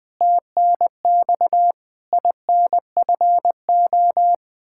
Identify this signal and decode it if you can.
Morse code, heard as a keyed tone, about 20 wpm, 700 Hz. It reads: TNX INFO
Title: Practice QSO - Exchange 3 - More Info - 20wpm